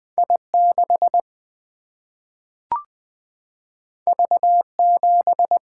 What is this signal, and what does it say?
Morse code, heard as a keyed tone, about 20 wpm, 700 Hz. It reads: I 6 V 7
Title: Single-Letter Single-Number 20wpm